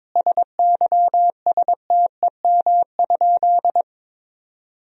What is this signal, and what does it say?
Morse code, heard as a keyed tone, about 22 wpm, 700 Hz. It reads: SYSTEM?
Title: Sentences from Top 1000 Words - Review Part 3 - 22wpm